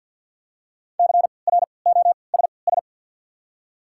Morse code, heard as a keyed tone, about 51 wpm, spaced as for 28 wpm, 700 Hz. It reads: CROSS
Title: 3x Character Spacing - Top 500 Words - 50wpm